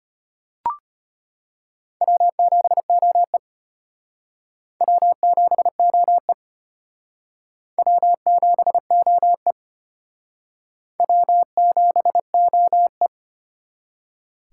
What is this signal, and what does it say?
Morse code, heard as a keyed tone, about 34 wpm, 700 Hz. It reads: W7OE W7OE W7OE W7OE